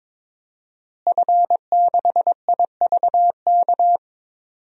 Morse code, heard as a keyed tone, about 22 wpm, 700 Hz. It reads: F6IVK
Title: DXCC-100 Call Signs - 5 Characters - Repeated 3x - 22wpm